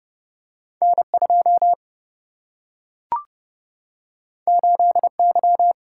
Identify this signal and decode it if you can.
Morse code, heard as a keyed tone, about 32 wpm, 700 Hz. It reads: N 2 8 Y
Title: Single-Letter Single-Number 30wpm